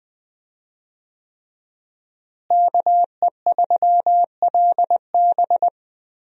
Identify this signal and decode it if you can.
Morse code, heard as a keyed tone, about 20 wpm, 700 Hz. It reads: KE3LB
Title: USA Call Signs - 5 Characters - Repeated 3x - 20wpm